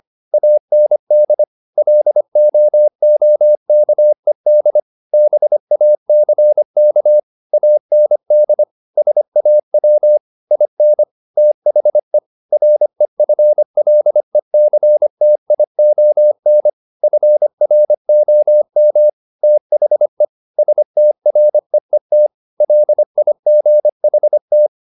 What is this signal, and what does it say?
Morse code, heard as a keyed tone, about 25 wpm, 600 Hz. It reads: AND LOOKED BACK AND SAW IN THE REFLECTION FROM THE STREET LIGHT